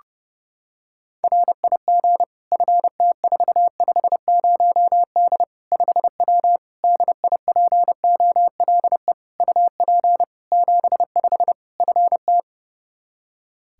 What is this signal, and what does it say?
Morse code, heard as a keyed tone, about 30 wpm, 700 Hz. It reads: RIG FT450D 5W DIPOLE UP 75 FT
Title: Practice QSO - Exchange 2 - Info - 30wpm